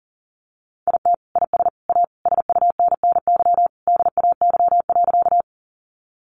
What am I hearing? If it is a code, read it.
Morse code, heard as a keyed tone, about 40 wpm, 700 Hz. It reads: IT IS A SUNNY DAY.
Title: Easy Phrases 40wpm